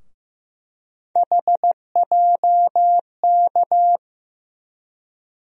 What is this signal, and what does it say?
Morse code, heard as a keyed tone, about 15 wpm, 700 Hz. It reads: HJK
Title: USA Call Sign Postfixes - 3 Letters 15wpm